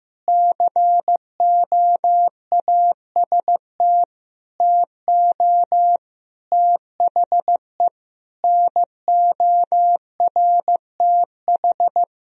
Morse code, heard as a keyed tone, about 15 wpm, 700 Hz. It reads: COAST TO THE NORTH
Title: Sentences from Top 1000 Words - Review Part 4 - 15wpm